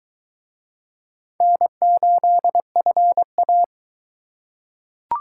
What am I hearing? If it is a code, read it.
Morse code, heard as a keyed tone, about 22 wpm, 700 Hz. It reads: N8FA